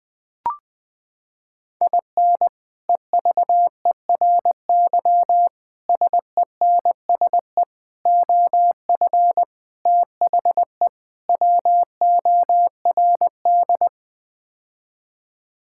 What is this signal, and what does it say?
Morse code, heard as a keyed tone, about 20 wpm, 700 Hz. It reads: IN EVERY SENSE OF THE WORD
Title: Sets of 6 Words - 20wpm